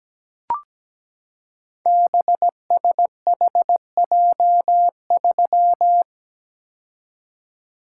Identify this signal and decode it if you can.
Morse code, heard as a keyed tone, about 17 wpm, 700 Hz. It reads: BSHJ3
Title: Code Groups - 5 Letter:Numbers - 17wpm